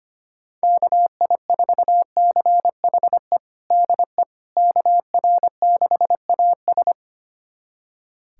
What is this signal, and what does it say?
Morse code, heard as a keyed tone, about 25 wpm, 700 Hz. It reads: KI4CHE DE KR6AS
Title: USA Callsign DE Callsign 25wpm